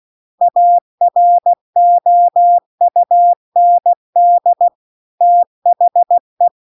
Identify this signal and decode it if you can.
Morse code, heard as a keyed tone, about 16 wpm, 700 Hz. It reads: AROUND THE